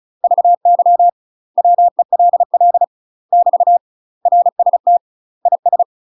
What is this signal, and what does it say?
Morse code, heard as a keyed tone, about 36 wpm, 700 Hz. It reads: VY WELL <BT> RST IS
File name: UK MASTER QSOs TRAINING/35WPM